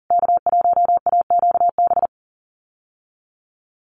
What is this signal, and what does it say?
Morse code, heard as a keyed tone, about 40 wpm, 700 Hz. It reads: K1AQB
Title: USA Call Signs Repeated 3x 40wpm